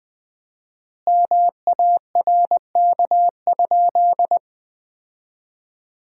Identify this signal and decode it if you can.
Morse code, heard as a keyed tone, about 21 wpm, 700 Hz. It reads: MARK?
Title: 4x Word-Spacing - Sentences from Top 300 Words - 20wpm